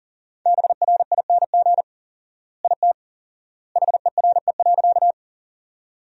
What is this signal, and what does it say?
Morse code, heard as a keyed tone, about 40 wpm, 700 Hz. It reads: BRING IT HERE.
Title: 4x Word-Spacing - Sentences from Top 500 Words - 40wpm